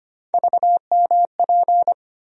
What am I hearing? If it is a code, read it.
Morse code, heard as a keyed tone, about 25 wpm, 700 Hz. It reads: VMP
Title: USA Call Sign Postfixes - 3 Letters 25wpm